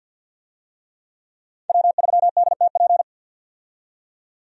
Morse code, heard as a keyed tone, about 48 wpm, 700 Hz. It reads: W3DTP
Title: USA Call Signs - 5 Characters - 50wpm